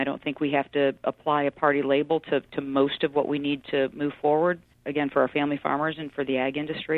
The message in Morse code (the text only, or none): none